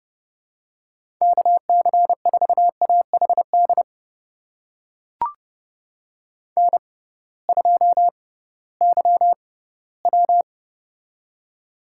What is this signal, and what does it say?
Morse code, heard as a keyed tone, about 30 wpm, spaced as for 18 wpm, 700 Hz. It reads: KC4AHD N2YW
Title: USA Call Signs - 6x Character Spacing - 30wpm